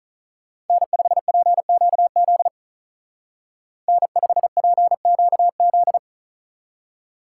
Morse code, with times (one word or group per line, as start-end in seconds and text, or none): 0.7-2.5 N5PQZ
3.9-6.0 N5PQZ